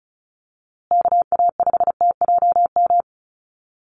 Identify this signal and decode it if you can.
Morse code, heard as a keyed tone, about 34 wpm, 700 Hz. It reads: KA5TJM
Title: USA Call Signs Repeated 3x 35wpm